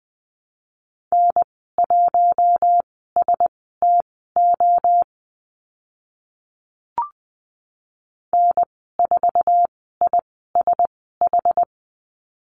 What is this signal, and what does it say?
Morse code, heard as a keyed tone, about 20 wpm, spaced as for 14 wpm, 700 Hz. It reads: N1STO N4ISH
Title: 062 Lesson 11g - Letter h - Call Signs 20wpm